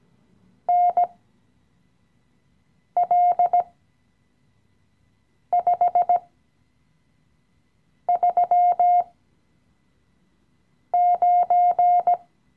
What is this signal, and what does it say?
Morse code, heard as a keyed tone, about 17 wpm, spaced as for 4 wpm, 710 Hz. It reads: NL539
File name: CW-Online-Kurs / Anfänger / Lektion 016